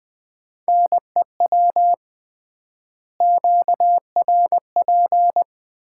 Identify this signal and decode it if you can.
Morse code, heard as a keyed tone, about 21 wpm, 700 Hz. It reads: NEW QRP
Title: Common QSO Phrases - 3x Word-Spacing - 20wpm